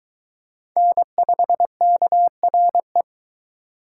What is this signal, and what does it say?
Morse code, heard as a keyed tone, about 22 wpm, 700 Hz. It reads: N5KRE